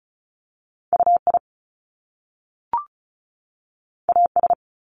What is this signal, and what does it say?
Morse code, heard as a keyed tone, about 36 wpm, 700 Hz. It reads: UI AS
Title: Top Two Letter Combinations in Words 35wpm